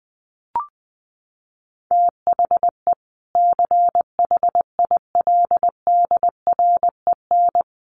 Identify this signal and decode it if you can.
Morse code, heard as a keyed tone, about 20 wpm, 700 Hz. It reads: THE CHILDREN
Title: Sentences from Top 300 Words 20wpm